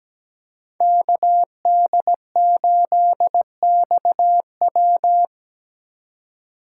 Morse code, heard as a keyed tone, about 17 wpm, 700 Hz. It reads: KD8XW
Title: USA Call Signs - 5 Characters - 3x Character Spacing - 17wpm